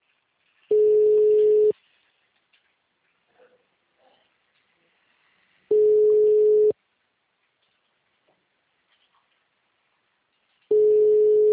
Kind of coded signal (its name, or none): telephone call progress tone